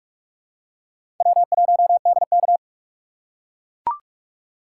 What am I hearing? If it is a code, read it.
Morse code, heard as a keyed tone, about 43 wpm, 700 Hz. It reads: W1DK
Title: USA Call Signs - 4 Characters - 3x Character Spacing - 45wpm